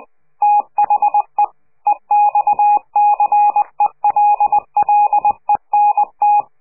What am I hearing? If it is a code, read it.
Morse code, heard as a keyed tone, about 20 wpm, 830 Hz. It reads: THE EXCELLENT
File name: CQ de PCH laatste bericht farewell